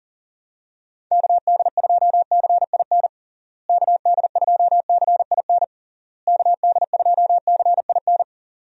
Morse code, heard as a keyed tone, about 40 wpm, 700 Hz. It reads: KD2CIN KD2CIN KD2CIN
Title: USA Call Signs - 6 Characters - Repeated 3x - 40wpm